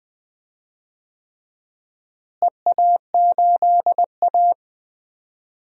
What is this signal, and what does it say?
Morse code, heard as a keyed tone, about 20 wpm, 700 Hz. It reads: EA8A
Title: DXCC-100 Call Signs - 4 Characters - Repeated 3x - 20wpm